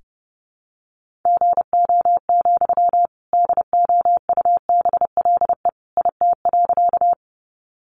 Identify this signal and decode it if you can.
Morse code, heard as a keyed tone, about 28 wpm, 700 Hz. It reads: GO, DOUBLE IT.
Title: Sentences from Top 1400 Words 30wpm